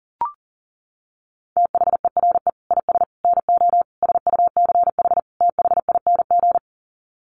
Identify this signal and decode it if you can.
Morse code, heard as a keyed tone, about 40 wpm, 700 Hz. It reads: THERE IS NO SUCH THING
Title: Sets of 5 Words Encore 40wpm